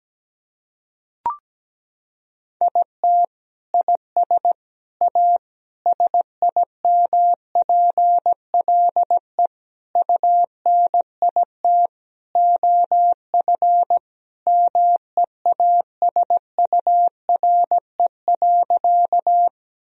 Morse code, heard as a keyed tone, about 17 wpm, 700 Hz. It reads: IT IS A SIMPLE UNIT OF MEASURE.
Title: Sentences from Top 1000 Words - Review Part 2 - 17wpm